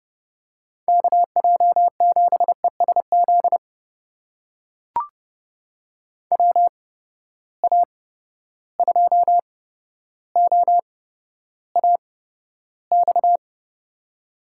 Morse code, heard as a keyed tone, about 30 wpm, spaced as for 13 wpm, 700 Hz. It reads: KJ7ESZ WA2OAX
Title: USA Call Signs - 6 Characters - 8x Character Spacing - 30wpm